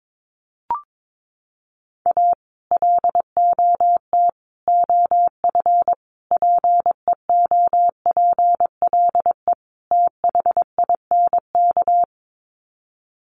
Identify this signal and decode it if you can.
Morse code, heard as a keyed tone, about 22 wpm, 700 Hz. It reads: A LOT OF PEOPLE THINK
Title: Sets of 5 Words Encore 22wpm